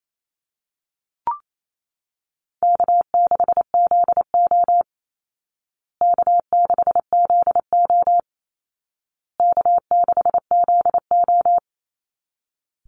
Morse code, heard as a keyed tone, about 28 wpm, 700 Hz. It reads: K6ZO K6ZO K6ZO
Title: Easy Call Signs Repeated 3x - 28wpm